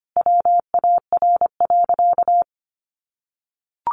Morse code, heard as a keyed tone, about 25 wpm, 700 Hz. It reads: WAR.
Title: Sentences from Top 1300 Words 25wpm